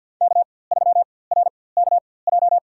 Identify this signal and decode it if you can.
Morse code, heard as a keyed tone, about 53 wpm, spaced as for 24 wpm, 700 Hz. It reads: K3RKJ
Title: USA Call Signs - 5 Characters - 4x Character Spacing - 50wpm